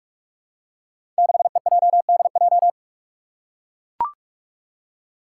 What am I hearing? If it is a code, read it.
Morse code, heard as a keyed tone, about 44 wpm, 700 Hz. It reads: 6EJDJ
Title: Five Challenging Letter-Number Combinations 45wpm